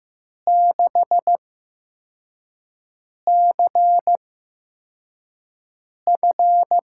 Morse code, heard as a keyed tone, about 15 wpm, spaced as for 4 wpm, 700 Hz. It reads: 6CF
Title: USA Call Signs - 5 Characters - 8x Character Spacing - 15wpm